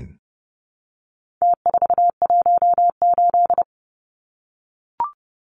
Morse code, heard as a keyed tone, about 30 wpm, 700 Hz. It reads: T418
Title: Numbers - Contest Style - 4-Digit 30wpm